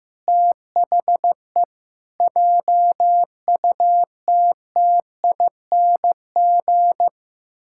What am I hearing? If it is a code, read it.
Morse code, heard as a keyed tone, about 15 wpm, 700 Hz. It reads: THE JUTTING
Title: Pangram Challenge v2 - 15wpm